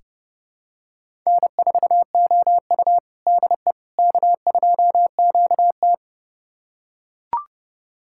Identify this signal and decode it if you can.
Morse code, heard as a keyed tone, about 30 wpm, 700 Hz. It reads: N4OU DE K2QT
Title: USA Callsign DE Callsign - 4 Characters - 30wpm